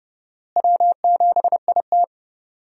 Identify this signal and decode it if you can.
Morse code, heard as a keyed tone, about 29 wpm, 700 Hz. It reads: W7IT
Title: USA Call Signs - 4 Characters - Repeated 3x - 30wpm